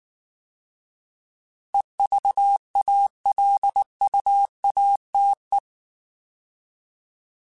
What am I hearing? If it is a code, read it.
Morse code, heard as a keyed tone, about 19 wpm, 800 Hz. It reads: EVALUATE
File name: Alphabetical / Alphabetical mp3s / Es @18